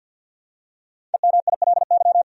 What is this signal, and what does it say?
Morse code, heard as a keyed tone, about 48 wpm, 700 Hz. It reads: EMILY
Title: USA Common Names - Max 5 Letters - 50wpm